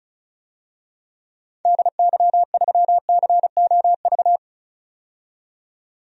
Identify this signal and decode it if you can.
Morse code, heard as a keyed tone, about 34 wpm, 700 Hz. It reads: DY3COV
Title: DXCC-100 Call Signs - 6 Characters - 35wpm